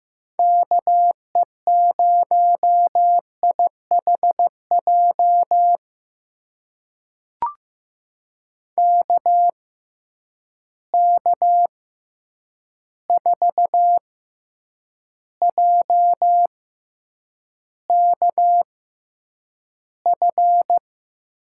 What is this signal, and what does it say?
Morse code, heard as a keyed tone, about 15 wpm, spaced as for 8 wpm, 700 Hz. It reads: KE0IHJ KK4JKF
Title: USA Call Signs - 6 Characters - 6x Character Spacing - 15wpm